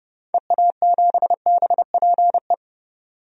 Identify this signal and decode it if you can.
Morse code, heard as a keyed tone, about 30 wpm, 700 Hz. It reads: EA7BPE
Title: DXCC-100 Call Signs - 6 Characters - 30wpm